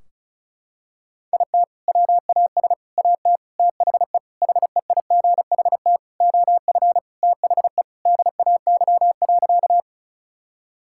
Morse code, heard as a keyed tone, about 34 wpm, 700 Hz. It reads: IT WAS AT THE HEIGHT OF THE DAY.
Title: Sentences from Top 1600 Words 35wpm